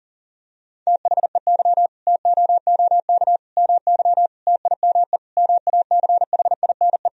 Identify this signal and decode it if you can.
Morse code, heard as a keyed tone, about 37 wpm, 700 Hz. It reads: THEY TOOK MY TIME MACHINE
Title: Sentences from Top 1000 Words - Review Part 2 - 40wpm